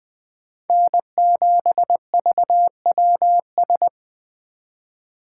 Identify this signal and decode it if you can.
Morse code, heard as a keyed tone, about 20 wpm, 700 Hz. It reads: N7VWS
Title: USA Call Signs - 5 Characters - 2x Character Spacing - 20wpm